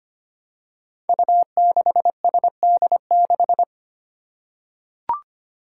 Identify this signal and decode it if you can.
Morse code, heard as a keyed tone, about 25 wpm, 700 Hz. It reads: U6SD6
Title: Five Challenging Letter-Number Combinations 25wpm